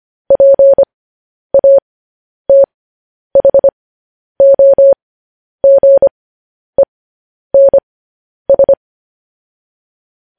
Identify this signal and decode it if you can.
Morse code, heard as a keyed tone, about 25 wpm, spaced as for 10 wpm, 550 Hz. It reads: PATHOGENS
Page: news-headlines-in-morse-code-at-10-wpm